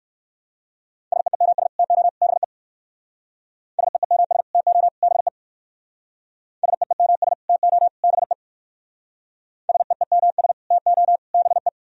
Morse code, heard as a keyed tone, about 58 wpm, 700 Hz. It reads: SEEMS TO BE SEEMS TO BE SEEMS TO BE SEEMS TO BE